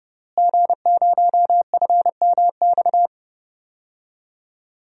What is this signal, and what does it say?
Morse code, heard as a keyed tone, about 30 wpm, 700 Hz. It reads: G0FMX
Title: DXCC-100 Call Signs - 5 Characters - Repeated 2x - 30wpm